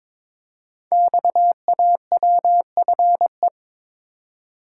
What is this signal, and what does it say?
Morse code, heard as a keyed tone, about 23 wpm, 700 Hz. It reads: XAWFE
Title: Code Groups - 5 Letters - 22wpm